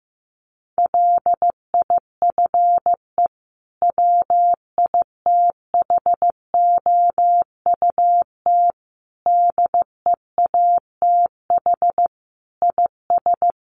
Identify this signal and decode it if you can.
Morse code, heard as a keyed tone, about 15 wpm, 700 Hz. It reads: LIFE WITHOUT DEATH IS
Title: Sentences from Top 1100 Words 15wpm